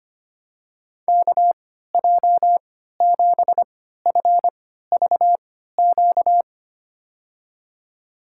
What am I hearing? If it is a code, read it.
Morse code, heard as a keyed tone, about 25 wpm, spaced as for 14 wpm, 700 Hz. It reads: KJ7FVQ
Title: USA Call Signs - 6 Characters - 3x Character Spacing - 25wpm